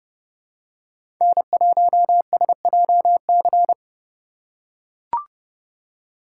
Morse code, heard as a keyed tone, about 30 wpm, 700 Hz. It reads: N1SJC
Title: USA Call Signs - 2x Character Spacing - 30wpm